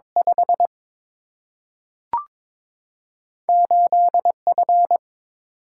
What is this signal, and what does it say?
Morse code, heard as a keyed tone, about 22 wpm, 700 Hz. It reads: 5 8 F
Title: Single-Letter Single-Number 22wpm